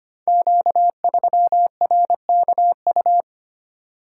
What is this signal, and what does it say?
Morse code, heard as a keyed tone, about 25 wpm, 700 Hz. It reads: Q3RKU
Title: Code Groups - 5 Letter:Numbers - 25wpm